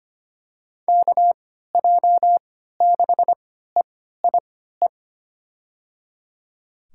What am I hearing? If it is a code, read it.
Morse code, heard as a keyed tone, about 25 wpm, spaced as for 14 wpm, 700 Hz. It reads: KJ6EIE
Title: USA Call Signs - 6 Characters - 3x Character Spacing - 25wpm